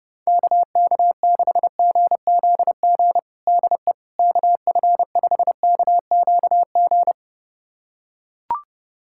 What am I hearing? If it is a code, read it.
Morse code, heard as a keyed tone, about 30 wpm, 700 Hz. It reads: KK6GZG DE KF5KQG
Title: USA Callsign DE Callsign - 6 Characters - 30wpm